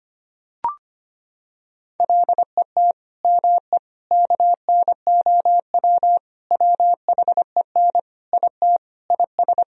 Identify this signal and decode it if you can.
Morse code, heard as a keyed tone, about 25 wpm, 700 Hz. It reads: LET ME KNOW WHEN IT IS
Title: Sentences from Top 1000 Words - Review Part 2 - 25wpm